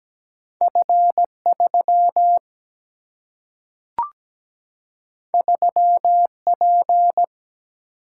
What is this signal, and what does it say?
Morse code, heard as a keyed tone, about 17 wpm, 700 Hz. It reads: F3 3P
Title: Single-Letter Single-Number 17wpm